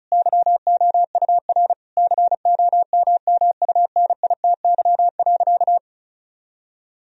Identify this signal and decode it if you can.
Morse code, heard as a keyed tone, about 36 wpm, 700 Hz. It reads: YOUR COMMUNITY.
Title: Sentences from Top 1000 Words - Review Part 4 - 35wpm